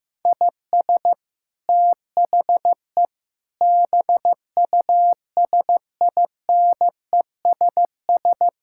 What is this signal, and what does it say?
Morse code, heard as a keyed tone, about 16 wpm, 700 Hz. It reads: IS THE BUSINESS
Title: Sentences from Top 1000 Words - Review Part 6 - 15wpm